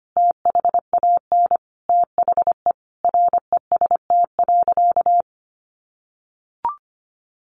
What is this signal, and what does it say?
Morse code, heard as a keyed tone, about 25 wpm, 700 Hz. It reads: THAN THE REST.
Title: Sentences from Top 1400 Words 25wpm